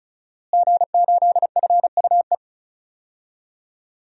Morse code, heard as a keyed tone, about 36 wpm, 700 Hz. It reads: G8FUE
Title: DXCC-100 Call Signs - 5 Characters - Repeated 3x - 35wpm